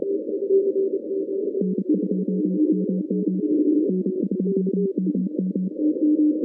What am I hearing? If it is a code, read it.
Morse code, heard as a keyed tone, about 28 wpm, 190 Hz. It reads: N2MM N2MM
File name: 2015 cq ww cw / 2015 cqwww k5tr - 0-18